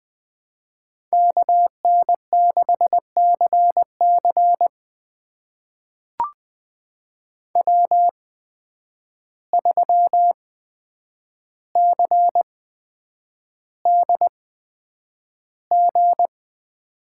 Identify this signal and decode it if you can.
Morse code, heard as a keyed tone, about 20 wpm, spaced as for 9 wpm, 700 Hz. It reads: KN6CC W3CDG
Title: USA Call Signs - 5 Characters - 8x Character Spacing - 20wpm